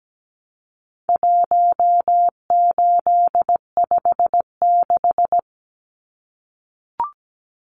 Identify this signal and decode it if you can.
Morse code, heard as a keyed tone, about 17 wpm, 700 Hz. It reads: 1856
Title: Numbers - 4-Digit 17wpm